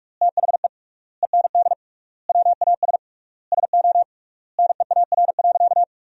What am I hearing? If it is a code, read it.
Morse code, heard as a keyed tone, about 46 wpm, 700 Hz. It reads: THE END WAS SO NEAR.
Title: 3x Word-Spacing - Sentences from Top 300 Words - 45wpm